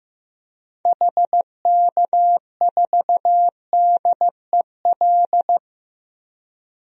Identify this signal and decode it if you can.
Morse code, heard as a keyed tone, about 15 wpm, 700 Hz. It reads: HK4DEL
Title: DXCC-100 Call Signs - 6 Characters - 15wpm